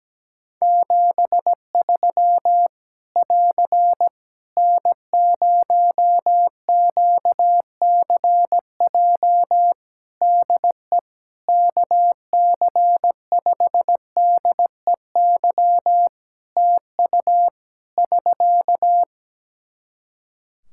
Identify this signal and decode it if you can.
Morse code, heard as a keyed tone, about 17 wpm, 700 Hz. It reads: 73 <AR> N0QCJ DE KC5DEY TU <SK>
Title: Practice QSO - Exchange 4 - Ending - 17wpm